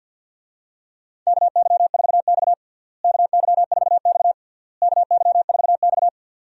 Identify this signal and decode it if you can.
Morse code, heard as a keyed tone, about 52 wpm, 700 Hz. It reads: KY4X KY4X KY4X
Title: USA Call Signs - 4 Characters - Repeated 3x - 50wpm